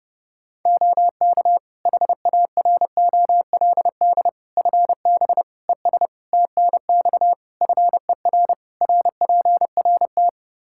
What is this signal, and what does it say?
Morse code, heard as a keyed tone, about 29 wpm, 700 Hz. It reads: OK HAROLD FB ES TNX FER RPRT
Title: Practice QSO - Exchange 2 - Info - 30wpm